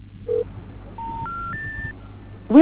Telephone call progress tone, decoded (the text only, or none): special information tone